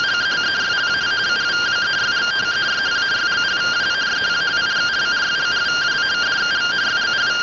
Two-tone teleprinter signal, about 170 Hz shift.